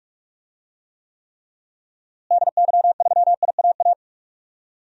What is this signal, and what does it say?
Morse code, heard as a keyed tone, about 45 wpm, 700 Hz. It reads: DY3IAA
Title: DXCC-100 Call Signs - 6 Characters - Repeated 3x - 45wpm